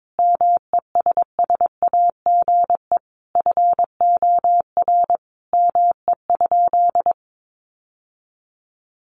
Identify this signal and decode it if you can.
Morse code, heard as a keyed tone, about 22 wpm, 700 Hz. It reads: MESSAGE FOR ME?
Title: Sentences from Top 1100 Words 22wpm